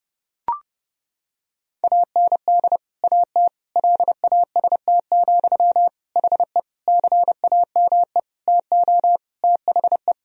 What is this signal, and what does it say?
Morse code, heard as a keyed tone, about 30 wpm, 700 Hz. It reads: AND AT LAST, HE CAME TO THE
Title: Sentences from Top 1000 Words - Review Part 2 - 30wpm